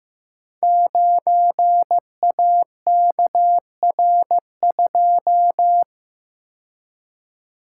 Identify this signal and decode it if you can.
Morse code, heard as a keyed tone, about 15 wpm, 700 Hz. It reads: 9AKR2
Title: Code Groups - 5 Letter:Numbers - 15wpm